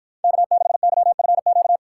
Morse code, heard as a keyed tone, about 49 wpm, 700 Hz. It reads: K6YFQ